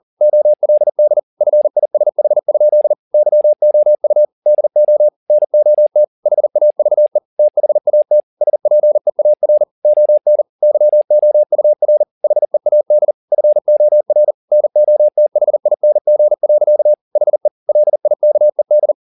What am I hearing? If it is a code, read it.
Morse code, heard as a keyed tone, about 40 wpm, 600 Hz. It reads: OLD FISH? YOU DO NOT HAVE THAT SPEAR ON YOUR HEAD FOR NOTHING. HE LIKED
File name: the old man and the sea 2 40wpm